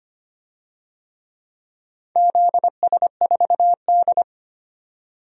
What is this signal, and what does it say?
Morse code, heard as a keyed tone, about 25 wpm, 700 Hz. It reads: ZS4D
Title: DXCC-100 Call Signs - 4 Characters - 25wpm